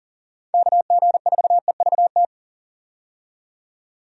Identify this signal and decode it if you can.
Morse code, heard as a keyed tone, about 40 wpm, 700 Hz. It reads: KG4EVT
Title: USA Call Signs - 6 Characters - 40wpm